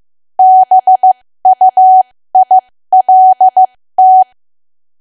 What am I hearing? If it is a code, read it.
Morse code, heard as a keyed tone, about 15 wpm, spaced as for 13 wpm, 750 Hz. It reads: BUILT
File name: cwa / W1AW-Level2 / 13WPM